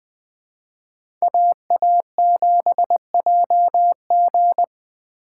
Morse code, heard as a keyed tone, about 20 wpm, 700 Hz. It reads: AA7JG